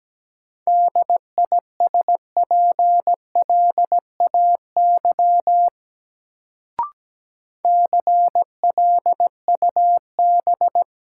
Morse code, heard as a keyed tone, about 17 wpm, 700 Hz. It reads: DISPLAY CLUB
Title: Sentences from Top 1200 Words - New Word Drill - 17wpm